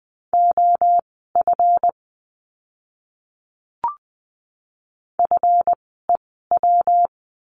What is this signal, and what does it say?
Morse code, heard as a keyed tone, about 20 wpm, spaced as for 15 wpm, 700 Hz. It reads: OF FEW
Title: 237 Lesson 38f - Letter q - Words 20wpm